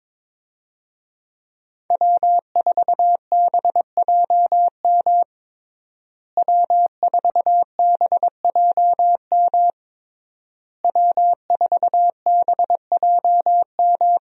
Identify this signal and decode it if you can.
Morse code, heard as a keyed tone, about 22 wpm, 700 Hz. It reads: W4BJM W4BJM W4BJM
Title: USA Call Signs - 5 Characters - Repeated 3x - 22wpm